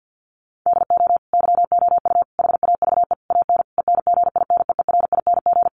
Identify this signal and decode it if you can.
Morse code, heard as a keyed tone, about 50 wpm, 700 Hz. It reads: DO YOU HAVE AN ENGINEERING
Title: Sentences from Top 1500 Words 50wpm